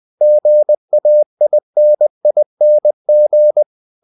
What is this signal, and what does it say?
Morse code, heard as a keyed tone, about 20 wpm, 600 Hz. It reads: GAINING
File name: the old man and the sea part 2 20wpm